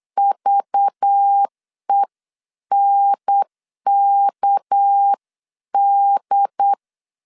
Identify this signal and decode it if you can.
Morse code, heard as a keyed tone, about 8 wpm, 800 Hz. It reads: VENKD
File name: CW / DET / 1 6 1500C S B 8wpm 5142